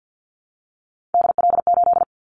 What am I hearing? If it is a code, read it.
Morse code, heard as a keyed tone, about 50 wpm, 700 Hz. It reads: DL8
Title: Callsign Prefixes for Top 25 DXCC - Repeated 3x - 50wpm